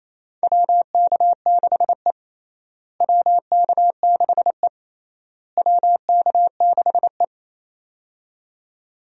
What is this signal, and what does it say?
Morse code, heard as a keyed tone, about 28 wpm, 700 Hz. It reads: WK6E WK6E WK6E
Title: USA Call Signs - 4 Characters - Repeated 3x - 28wpm